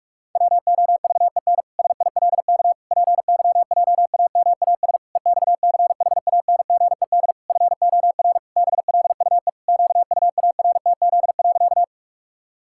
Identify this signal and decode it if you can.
Morse code, heard as a keyed tone, about 46 wpm, 700 Hz. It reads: WOVEN SILK PYJAMAS EXCHANGED FOR BLUE QUARTZ.
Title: Pangram Challenge v2 - 45wpm